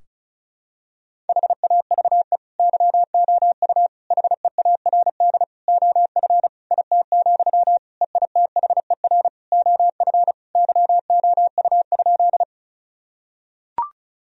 Morse code, heard as a keyed tone, about 36 wpm, 700 Hz. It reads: HAVE YOU HEARD OF IT, EITHER OF YOU?
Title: Sentences from Top 1000 Words - Review Part 4 - 35wpm